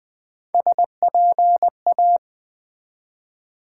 Morse code, heard as a keyed tone, about 20 wpm, 700 Hz. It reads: SPA
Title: Build A Word - 1 Letter At A Time - 3 to 5-Letter Words - 20wpm